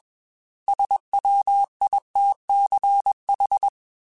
Morse code, heard as a keyed tone, about 21 wpm, 800 Hz. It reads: SWITCH
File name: Chuckles / Chuckles-mp3s / MiddleAge @20